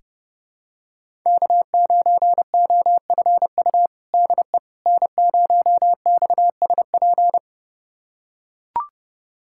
Morse code, heard as a keyed tone, about 30 wpm, 700 Hz. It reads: K9OFU DE N0XSP
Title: USA Callsign DE Callsign - 5 Characters - 30wpm